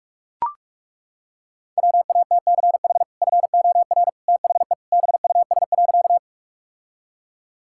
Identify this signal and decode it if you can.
Morse code, heard as a keyed tone, about 41 wpm, 700 Hz. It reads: WATCH FOR THE BUS.
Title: Sentences from Top 1000 Words - Review Part 6 - 45wpm